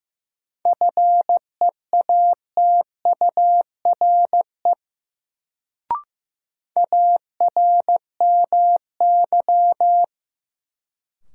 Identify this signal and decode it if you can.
Morse code, heard as a keyed tone, about 15 wpm, 700 Hz. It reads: FEATURE ARMY
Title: Sentences from Top 800 Words - New Word Drill - 15wpm